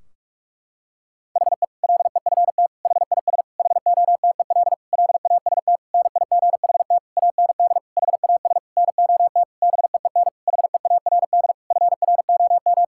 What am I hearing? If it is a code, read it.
Morse code, heard as a keyed tone, about 41 wpm, 700 Hz. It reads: HE LEFT HIS HOTEL LAST NIGHT AND HAS NOT BEEN HEARD FROM
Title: Sentences from Top 1600 Words 45wpm